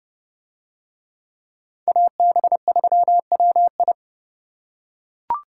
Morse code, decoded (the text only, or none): AB3WI